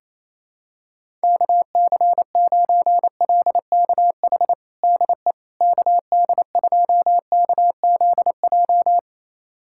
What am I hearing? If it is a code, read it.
Morse code, heard as a keyed tone, about 28 wpm, 700 Hz. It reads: KC9LKH DE KD2KZJ
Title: USA Callsign DE Callsign 28wpm